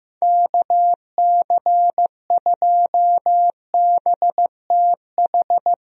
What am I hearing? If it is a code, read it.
Morse code, heard as a keyed tone, about 15 wpm, 700 Hz. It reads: KC2BTH
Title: USA Call Signs - 6 Characters - Repeated 2x - 15wpm